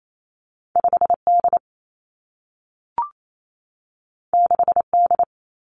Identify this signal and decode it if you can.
Morse code, heard as a keyed tone, about 28 wpm, 700 Hz. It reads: 5 D 6 D
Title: Two Challenging Letter-Number Combinations 28wpm